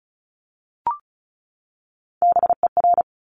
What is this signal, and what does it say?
Morse code, heard as a keyed tone, about 36 wpm, 700 Hz. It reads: BER
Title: Top Three Letter Combinations in Words 35wpm